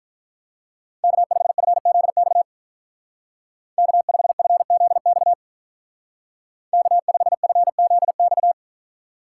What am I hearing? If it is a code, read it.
Morse code, heard as a keyed tone, about 43 wpm, 700 Hz. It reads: K5FZX K5FZX K5FZX
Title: Speed Racing - USA 1x3 Call Signs 35wpm